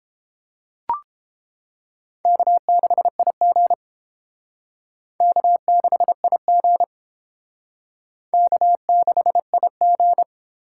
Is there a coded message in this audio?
Yes: Morse code from 2.3 to 10.2 s, heard as a keyed tone, about 29 wpm, 700 Hz. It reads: K6IG K6IG K6IG